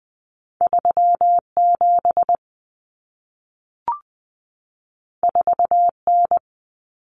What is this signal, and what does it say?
Morse code, heard as a keyed tone, about 20 wpm, 700 Hz. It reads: 37 4N
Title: Common Cut Numbers - 2-Digit 20wpm